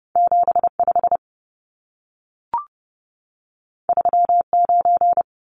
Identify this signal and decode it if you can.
Morse code, heard as a keyed tone, about 30 wpm, 700 Hz. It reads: 75 39
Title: Numbers - 2-Digit 30wpm